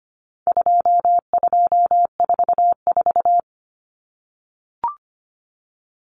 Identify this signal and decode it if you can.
Morse code, heard as a keyed tone, about 25 wpm, 700 Hz. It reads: 2244
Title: Common Cut Numbers - 4-Digit 25wpm